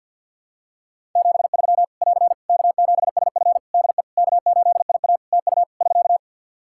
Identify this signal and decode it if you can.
Morse code, heard as a keyed tone, about 47 wpm, 700 Hz. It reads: 73 <AR> K7SF DE K8IA TU <SK>
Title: Practice QSO - Exchange 4 - Ending - 50wpm